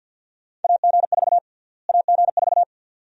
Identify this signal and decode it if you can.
Morse code, heard as a keyed tone, about 51 wpm, 700 Hz. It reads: AG4 AG4
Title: USA Call Sign Prefixes - Repeated 2x - 50wpm